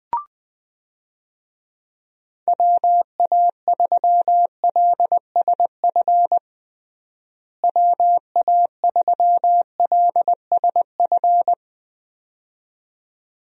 Morse code, heard as a keyed tone, about 21 wpm, 700 Hz. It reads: WA3LSF WA3LSF
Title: USA Call Signs - 6 Characters - Repeated 2x - 20wpm